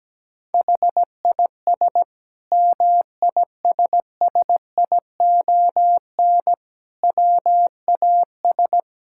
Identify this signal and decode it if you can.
Morse code, heard as a keyed tone, about 17 wpm, 700 Hz. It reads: HIS MISSION WAS
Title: Sentences from Top 1600 Words 17wpm